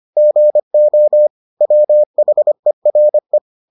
Morse code, heard as a keyed tone, about 25 wpm, 600 Hz. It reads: GO WHERE